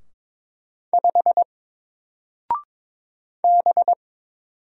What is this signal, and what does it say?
Morse code, heard as a keyed tone, about 22 wpm, 700 Hz. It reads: 5 B
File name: Single-Letter-Number-22wpm